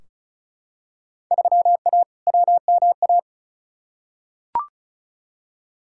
Morse code, heard as a keyed tone, about 34 wpm, 700 Hz. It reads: 3A WMA
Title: Field Day Exchanges - 35wpm